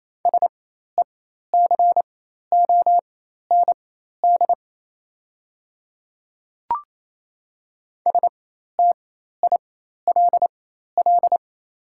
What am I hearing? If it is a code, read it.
Morse code, heard as a keyed tone, about 28 wpm, spaced as for 13 wpm, 700 Hz. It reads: SECOND STILL
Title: 4x Character Spacing - Top 300 Words - 28wpm